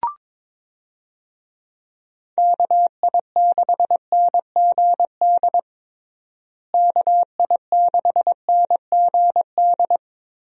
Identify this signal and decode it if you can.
Morse code, heard as a keyed tone, about 22 wpm, 700 Hz. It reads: KI6NGD KI6NGD
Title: USA Call Signs - 6 Characters - Repeated 3x - 22wpm